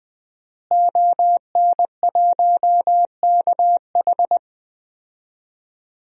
Morse code, heard as a keyed tone, about 20 wpm, 700 Hz. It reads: ON1KH
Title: DXCC-100 Call Signs - 5 Characters - Repeated 3x - 20wpm